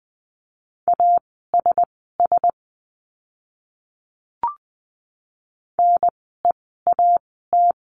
Morse code, heard as a keyed tone, about 20 wpm, spaced as for 14 wpm, 700 Hz. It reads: ASS NEAT
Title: 036 Lesson 7f - Letter s - Words 20wpm